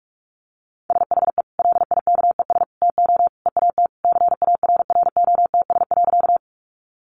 Morse code, heard as a keyed tone, about 44 wpm, 700 Hz. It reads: SHE LIKES TO EAT CARROTS.
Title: Easy Phrases 45wpm